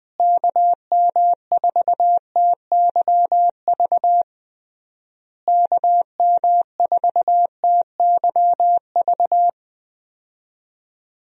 Morse code, heard as a keyed tone, about 20 wpm, 700 Hz. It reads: KM4TYV KM4TYV
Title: USA Call Signs - 6 Characters - Repeated 3x - 20wpm